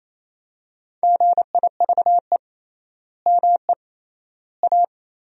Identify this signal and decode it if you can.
Morse code, heard as a keyed tone, about 28 wpm, 700 Hz. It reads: GIVE ME A
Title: 3x Word-Spacing - Sentences from Top 200 Words - 28wpm